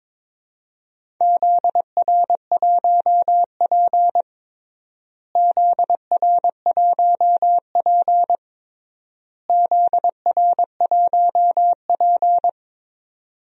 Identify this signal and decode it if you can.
Morse code, heard as a keyed tone, about 22 wpm, 700 Hz. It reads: ZR1P ZR1P ZR1P
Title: DXCC-100 Call Signs - 4 Characters - Repeated 3x - 22wpm